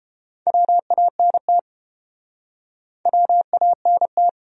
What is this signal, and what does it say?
Morse code, heard as a keyed tone, about 29 wpm, 700 Hz. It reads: WANT WANT